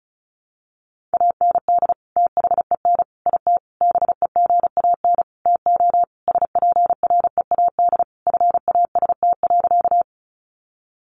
Morse code, heard as a keyed tone, about 34 wpm, 700 Hz. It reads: AND THEN IT BEGAN TO SPREAD FAST.
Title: Sentences from Top 1200 Words 35wpm